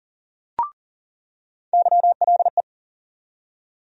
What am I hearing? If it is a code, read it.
Morse code, heard as a keyed tone, about 40 wpm, 700 Hz. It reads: YLE
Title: USA Call Sign Postfixes - 3 Letters 40wpm